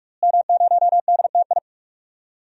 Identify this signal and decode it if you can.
Morse code, heard as a keyed tone, about 46 wpm, 700 Hz. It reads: M0DTI
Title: DXCC-100 Call Signs - 5 Characters - Repeated 3x - 45wpm